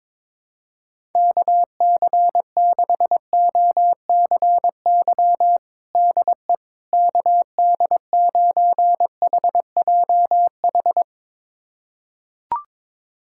Morse code, heard as a keyed tone, about 22 wpm, 700 Hz. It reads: KC6OCY DE KD9HJH
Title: USA Callsign DE Callsign - 6 Characters - 22wpm